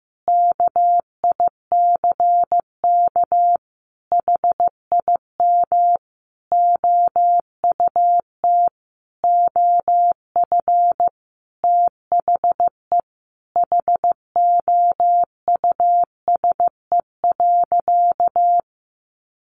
Morse code, heard as a keyed tone, about 14 wpm, 700 Hz. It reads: KICK HIM OUT OF THE HOUSE.
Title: Sentences from Top 1300 Words 15wpm